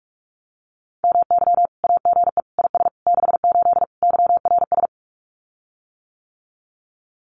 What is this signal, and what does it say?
Morse code, heard as a keyed tone, about 44 wpm, 700 Hz. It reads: MY AGE IS 68 YRS
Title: Common QSO Phrases Encore 45wpm